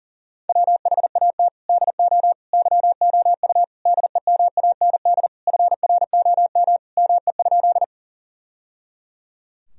Morse code, heard as a keyed tone, about 40 wpm, 700 Hz. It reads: WHAT DO YOU DEMAND FROM ME?
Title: Sentences from Top 1000 Words - Review Part 4 - 40wpm